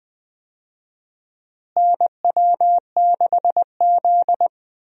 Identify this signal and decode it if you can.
Morse code, heard as a keyed tone, about 20 wpm, 700 Hz. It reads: NW6Z
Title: USA Call Signs - 4 Characters - Repeated 3x - 20wpm